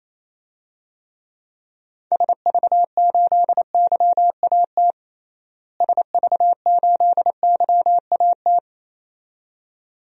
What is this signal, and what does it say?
Morse code, heard as a keyed tone, about 28 wpm, 700 Hz. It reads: SV8YAT SV8YAT
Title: DXCC-100 Call Signs - 6 Characters - Repeated 2x - 28wpm